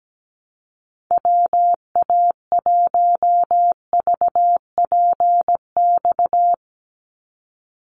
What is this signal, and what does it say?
Morse code, heard as a keyed tone, about 17 wpm, 700 Hz. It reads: WA1VPX
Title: USA Call Signs 17wpm